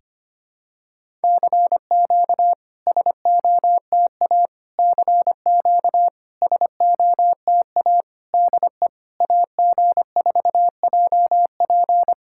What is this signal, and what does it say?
Morse code, heard as a keyed tone, about 25 wpm, 690 Hz. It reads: CQ SOTA CQ SOTA DE AG4JP
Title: SOTA QSO Practice 25wpm